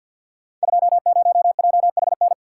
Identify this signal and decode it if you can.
Morse code, heard as a keyed tone, about 50 wpm, 700 Hz. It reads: 20JHN
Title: Code Groups - 5 Letter:Numbers - 50wpm